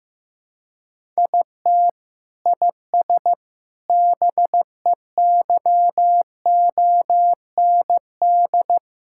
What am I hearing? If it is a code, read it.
Morse code, heard as a keyed tone, about 15 wpm, 700 Hz. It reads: IT IS BEYOND
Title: Sentences from Top 1000 Words - Review Part 6 - 15wpm